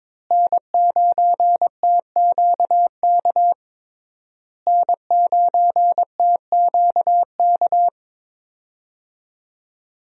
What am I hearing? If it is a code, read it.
Morse code, heard as a keyed tone, about 22 wpm, 700 Hz. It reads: N9TQK N9TQK
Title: USA Call Signs - 5 Characters - Repeated 3x - 22wpm